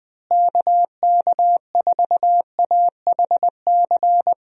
Morse code, heard as a keyed tone, about 20 wpm, 700 Hz. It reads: KK4AHC
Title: USA Call Signs - 6 Characters - Repeated 3x - 20wpm